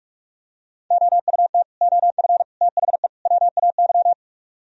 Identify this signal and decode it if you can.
Morse code, heard as a keyed tone, about 37 wpm, 700 Hz. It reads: OUT OF THE WAY
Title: Speed-Racing - Sets of 4 Words 30wpm